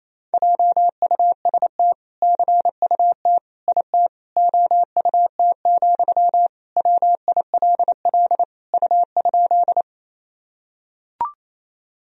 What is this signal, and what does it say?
Morse code, heard as a keyed tone, about 28 wpm, 700 Hz. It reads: JUST CUT IT OUT, WILL U?
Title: Sentences from Top 300 Words and Abbreviations 28wpm